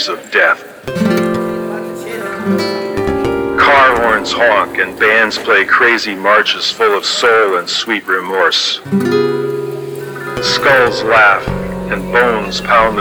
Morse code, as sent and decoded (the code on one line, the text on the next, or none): none